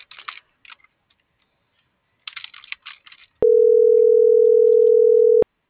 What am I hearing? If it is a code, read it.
Ringback tone (North America), 440 + 480 Hz.